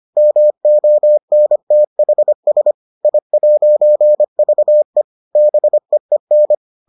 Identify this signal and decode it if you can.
Morse code, heard as a keyed tone, about 25 wpm, 600 Hz. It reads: MONTHS I'VE BEEN